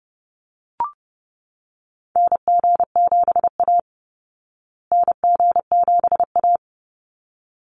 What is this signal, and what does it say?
Morse code, heard as a keyed tone, about 30 wpm, 700 Hz. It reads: NG7A NG7A
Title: Call Signs Repeated 3x - 30wpm